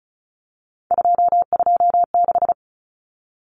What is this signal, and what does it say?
Morse code, heard as a keyed tone, about 34 wpm, 700 Hz. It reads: 226
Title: Common Cut Numbers - 3-Digit 35wpm